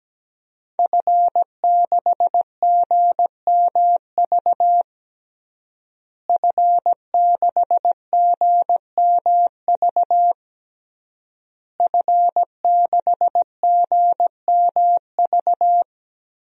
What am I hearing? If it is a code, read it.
Morse code, heard as a keyed tone, about 17 wpm, 700 Hz. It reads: F6GMV F6GMV F6GMV
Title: DXCC-100 Call Signs - 5 Characters - Repeated 3x - 17wpm